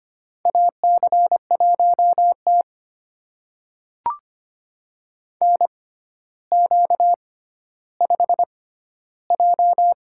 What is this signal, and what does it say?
Morse code, heard as a keyed tone, about 25 wpm, spaced as for 13 wpm, 700 Hz. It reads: AC1T NQ5J
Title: USA Call Signs - 4 Characters - 6x Character Spacing - 25wpm